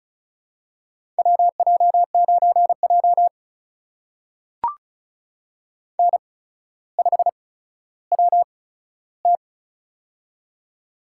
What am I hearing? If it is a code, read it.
Morse code, heard as a keyed tone, about 36 wpm, spaced as for 15 wpm, 700 Hz. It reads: WJ9J N5WT
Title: USA Call Signs - 4 Characters - 8x Character Spacing - 35wpm